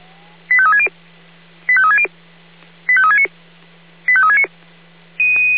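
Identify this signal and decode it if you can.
Five-tone selective call (ZVEI), heard as quick runs of five tones, tones of about 70 ms. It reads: 85360 85360 85369 85369
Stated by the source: ZVEI five-tone selective call received on SDR.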